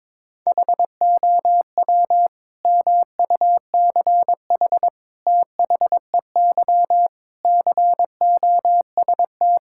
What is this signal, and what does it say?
Morse code, heard as a keyed tone, about 22 wpm, 700 Hz. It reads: HOW MUCH THEY COST
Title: Sentences from Top 1000 Words - Review Part 5 - 22wpm